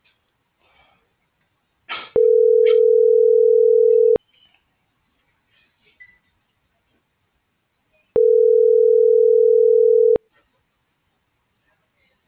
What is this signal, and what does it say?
Ringback tone (North America), 440 + 480 Hz.
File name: CENTRAL GOMSA / out alan.escudero